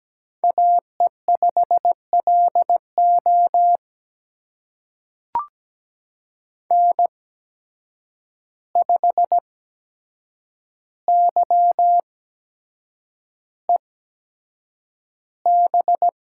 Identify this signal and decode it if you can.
Morse code, heard as a keyed tone, about 17 wpm, spaced as for 7 wpm, 700 Hz. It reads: AE5LO N5YEB
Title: USA Call Signs - 5 Characters - 8x Character Spacing - 17wpm